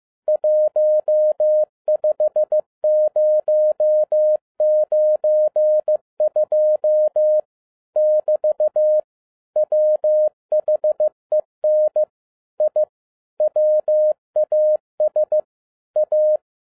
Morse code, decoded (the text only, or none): 15092 <BT> WHEN I WAS A